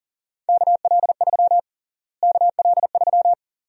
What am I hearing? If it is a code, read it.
Morse code, heard as a keyed tone, about 37 wpm, 700 Hz. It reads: KL3 KL3
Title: USA Call Sign Prefixes - Repeated 2x - 40wpm